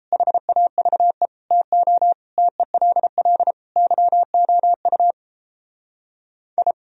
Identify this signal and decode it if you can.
Morse code, heard as a keyed tone, about 35 wpm, 700 Hz. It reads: HAVE TO TELL YOU I